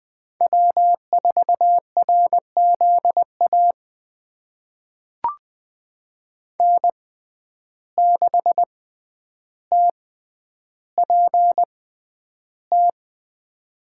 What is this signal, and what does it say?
Morse code, heard as a keyed tone, about 20 wpm, spaced as for 10 wpm, 700 Hz. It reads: W4RZA N6TPT
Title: USA Call Signs - 5 Characters - 6x Character Spacing - 20wpm